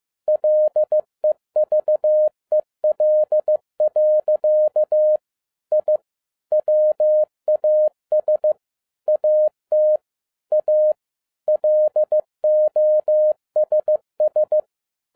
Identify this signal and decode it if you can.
Morse code, heard as a keyed tone, about 15 wpm, 610 Hz. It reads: LEVEL. I WAS AT A LOSS
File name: MY CW LIFE 1 15wpm